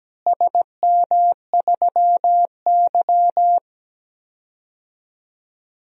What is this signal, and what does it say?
Morse code, heard as a keyed tone, about 17 wpm, 700 Hz. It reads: SM3Y
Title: DXCC-100 Call Signs - 4 Characters - Repeated 2x - 17wpm